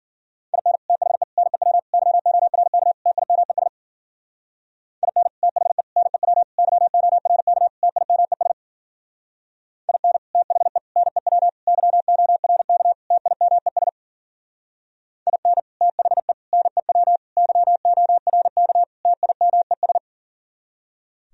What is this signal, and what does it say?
Morse code, heard as a keyed tone, about 50 wpm, 700 Hz. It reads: IN THE NEW YORK TIMES IN THE NEW YORK TIMES IN THE NEW YORK TIMES IN THE NEW YORK TIMES
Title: Speed-Racing - Sets of 5 Words 40wpm